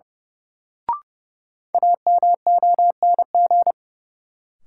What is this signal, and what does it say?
Morse code, heard as a keyed tone, about 30 wpm, 700 Hz. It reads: AMONG
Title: Sets of 2 Words - New Word Drill - 30wpm